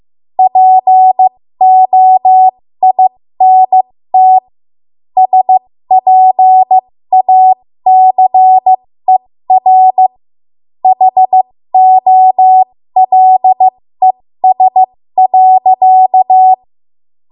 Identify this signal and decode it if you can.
Morse code, heard as a keyed tone, about 15 wpm, spaced as for 13 wpm, 750 Hz. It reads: POINT SPACER HOLES.
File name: Morse / 13WPM